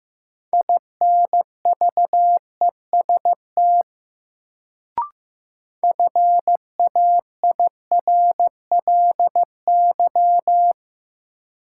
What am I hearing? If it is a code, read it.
Morse code, heard as a keyed tone, about 16 wpm, 700 Hz. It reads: INVEST FAIRLY
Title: Sentences from Top 1100 Words - New Word Drill - 15wpm